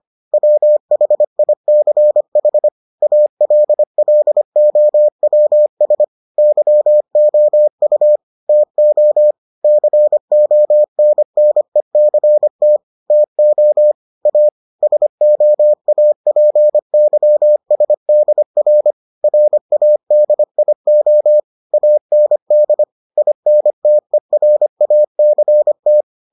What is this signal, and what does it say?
Morse code, heard as a keyed tone, about 25 wpm, 600 Hz. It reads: WHICH ALLOWS YOU TO CONNECT TO A SOAPYSDR RADIO AND INTERACT